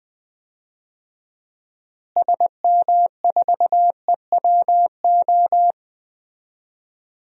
Morse code, heard as a keyed tone, about 20 wpm, 700 Hz. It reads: SM4EWO
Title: DXCC-100 Call Signs - 6 Characters - Repeated 3x - 20wpm